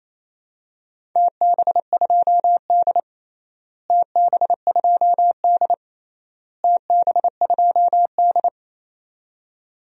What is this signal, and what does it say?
Morse code, heard as a keyed tone, about 28 wpm, 700 Hz. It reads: TB2D TB2D TB2D
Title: DXCC-100 Call Signs - 4 Characters - Repeated 3x - 28wpm